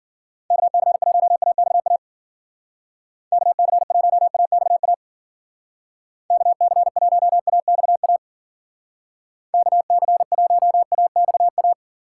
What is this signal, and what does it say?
Morse code, heard as a keyed tone, about 50 wpm, 700 Hz. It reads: KC1AXA KC1AXA KC1AXA KC1AXA